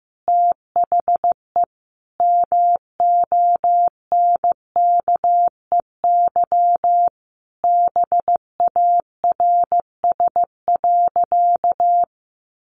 Morse code, heard as a keyed tone, about 15 wpm, 700 Hz. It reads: THE MONKEY BARS.
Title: Easy Phrases 15wpm